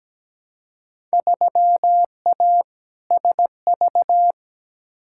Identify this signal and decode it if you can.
Morse code, heard as a keyed tone, about 17 wpm, 700 Hz. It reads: 3A SV
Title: Field Day Exchanges - 17wpm